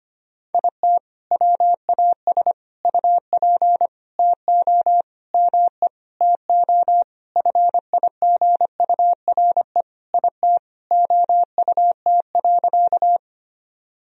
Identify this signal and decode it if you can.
Morse code, heard as a keyed tone, about 25 wpm, 700 Hz. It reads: IT WAS UP TO ME TO FIGURE IT OUT.
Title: Sentences from Top 1000 Words - Review Part 2 - 25wpm